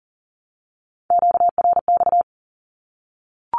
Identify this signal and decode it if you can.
Morse code, heard as a keyed tone, about 38 wpm, 700 Hz. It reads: QRX
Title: Q-Codes Repeated 3x 40wpm